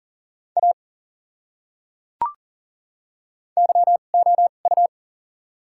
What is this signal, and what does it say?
Morse code, heard as a keyed tone, about 40 wpm, spaced as for 29 wpm, 700 Hz. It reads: A YOU
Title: 2x Character Spacing - Top 100 Words - 40wpm